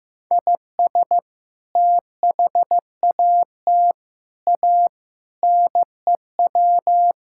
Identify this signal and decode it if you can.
Morse code, heard as a keyed tone, about 15 wpm, 700 Hz. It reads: IS THAT A NEW
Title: Sentences from Top 1000 Words - Review Part 5 - 15wpm